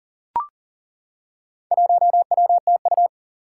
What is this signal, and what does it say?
Morse code, heard as a keyed tone, about 40 wpm, 700 Hz. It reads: 1WTU
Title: Four Challenging Letter-Number Combinations 40wpm